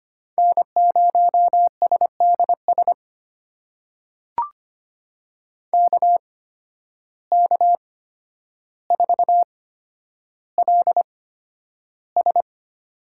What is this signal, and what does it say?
Morse code, heard as a keyed tone, about 25 wpm, spaced as for 11 wpm, 700 Hz. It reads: N0SDS KK4LS
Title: USA Call Signs - 5 Characters - 8x Character Spacing - 25wpm